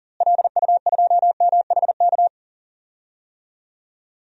Morse code, heard as a keyed tone, about 40 wpm, 700 Hz. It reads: LU2MHK
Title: DXCC-100 Call Signs - 6 Characters - Repeated 3x - 40wpm